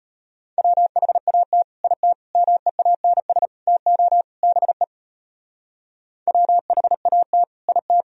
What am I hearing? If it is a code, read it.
Morse code, heard as a keyed tone, about 38 wpm, 700 Hz. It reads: WHAT IT MEANS TO BE WHAT IT